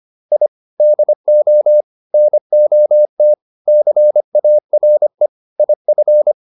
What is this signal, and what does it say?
Morse code, heard as a keyed tone, about 25 wpm, 600 Hz. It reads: I DO NOT CARE IF